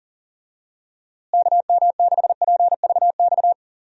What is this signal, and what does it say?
Morse code, heard as a keyed tone, about 40 wpm, 700 Hz. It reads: KM6PVX
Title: USA Call Signs - 6 Characters - Repeated 3x - 40wpm